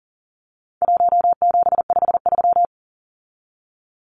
Morse code, heard as a keyed tone, about 40 wpm, 700 Hz. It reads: 1753
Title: Numbers - 4-Digit 40wpm